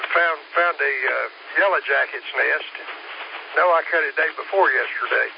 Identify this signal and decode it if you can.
Emergency Alert System (SAME) tones, heard as harsh buzzing bursts, with speech. It is heard from 0.8 to 4.3 s.